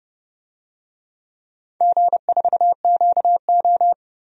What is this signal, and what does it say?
Morse code, heard as a keyed tone, about 32 wpm, 700 Hz. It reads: G4QO
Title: DXCC-100 Call Signs - 4 Characters - Repeated 3x - 30wpm